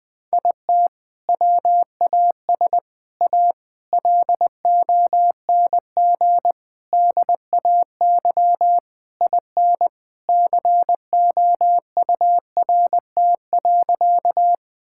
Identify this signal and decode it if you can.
Morse code, heard as a keyed tone, about 21 wpm, 700 Hz. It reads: IT WAS A LONG DAY IN COURT.
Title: Sentences from Top 1600 Words 20wpm